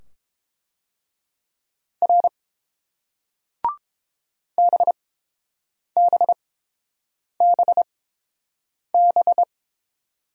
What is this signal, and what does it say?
Morse code, heard as a keyed tone, about 30 wpm, spaced as for 7 wpm, 700 Hz. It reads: R BBBB